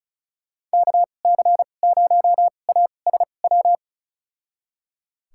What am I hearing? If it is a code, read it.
Morse code, heard as a keyed tone, about 36 wpm, spaced as for 25 wpm, 700 Hz. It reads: KC0ASW
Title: USA Call Signs - 2x Character Spacing - 35wpm